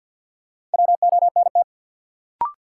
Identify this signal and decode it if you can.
Morse code, heard as a keyed tone, about 53 wpm, 700 Hz. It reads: WONT
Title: Sentences from Top 1000 Words - Review Part 6 - New Word Drill - 50wpm